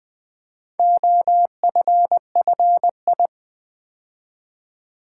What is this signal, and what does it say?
Morse code, heard as a keyed tone, about 20 wpm, 700 Hz. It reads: OFFI
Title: Build A Word - 2 Letters At A Time - 8-Letter Words - 20wpm